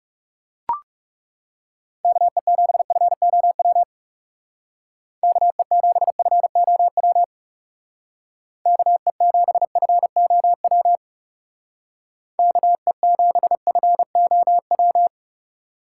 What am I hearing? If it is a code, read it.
Morse code, heard as a keyed tone, about 37 wpm, 700 Hz. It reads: KE7FOW KE7FOW KE7FOW KE7FOW
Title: Speed Racing - USA 2x3 Call Signs 30wpm